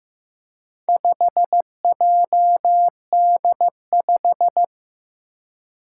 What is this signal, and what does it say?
Morse code, heard as a keyed tone, about 15 wpm, 700 Hz. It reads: 5JD5
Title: Four Challenging Letter-Number Combinations 15wpm